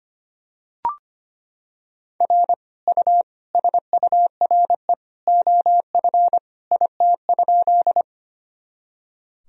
Morse code, heard as a keyed tone, about 25 wpm, 700 Hz. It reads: R U SURE OF IT?
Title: Sentences from Top 300 Words and Abbreviations 25wpm